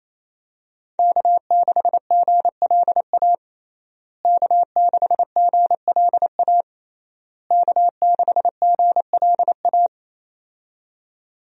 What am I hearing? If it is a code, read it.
Morse code, heard as a keyed tone, about 28 wpm, 700 Hz. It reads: K6GLA K6GLA K6GLA
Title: USA Call Signs - 5 Characters - Repeated 3x - 28wpm